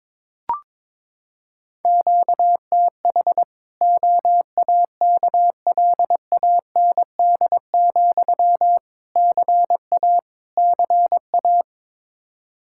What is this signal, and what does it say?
Morse code, heard as a keyed tone, about 22 wpm, 700 Hz. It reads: QTH OAKLAND, CA CA
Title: Practice QSO - Exchange 1 - Beginning - 22wpm